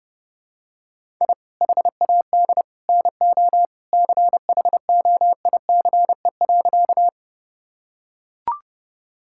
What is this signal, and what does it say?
Morse code, heard as a keyed tone, about 30 wpm, 700 Hz. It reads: I HAD NO CHOICE.
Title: Sentences from Top 1000 Words - Review Part 6 - 30wpm